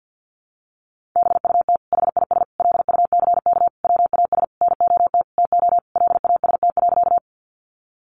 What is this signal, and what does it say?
Morse code, heard as a keyed tone, about 50 wpm, 700 Hz. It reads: BUT HIS LUCK WAS NOT TO LAST.
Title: Sentences from Top 1500 Words 50wpm